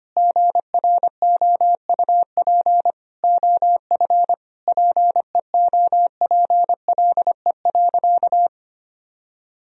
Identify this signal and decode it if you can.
Morse code, heard as a keyed tone, about 25 wpm, 700 Hz. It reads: GROUP OF PEOPLE.
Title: Sentences from Top 1000 Words - Review Part 1 - 25wpm